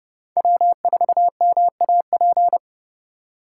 Morse code, heard as a keyed tone, about 29 wpm, 700 Hz. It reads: W4MAP